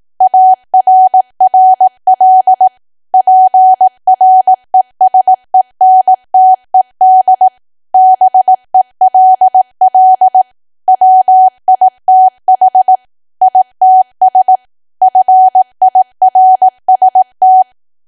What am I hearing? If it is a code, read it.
Morse code, heard as a keyed tone, about 18 wpm, 750 Hz. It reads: ARRL PRESENTED BELL WITH ITS FIRST
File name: Archive / 18 WPM / 18WPM